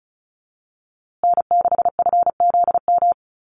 Morse code, heard as a keyed tone, about 34 wpm, 700 Hz. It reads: N6FZM
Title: USA Call Signs Repeated 2x 35wpm